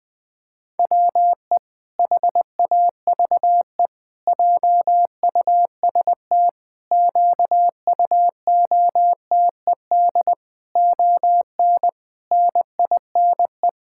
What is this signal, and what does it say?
Morse code, heard as a keyed tone, about 20 wpm, 700 Hz. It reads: WE HAVE JUST QUOTED ON NINE
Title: Pangram Challenge v2 - 20wpm